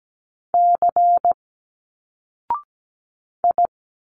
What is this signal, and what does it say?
Morse code, heard as a keyed tone, about 17 wpm, 700 Hz. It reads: C I
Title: Single Letters 17wpm